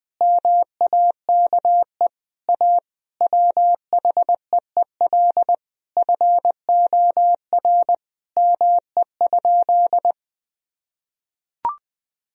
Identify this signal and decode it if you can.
Morse code, heard as a keyed tone, about 20 wpm, 700 Hz. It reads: MAKE A WHEEL FOR ME?
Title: Sentences from Top 1000 Words - Review Part 3 - 20wpm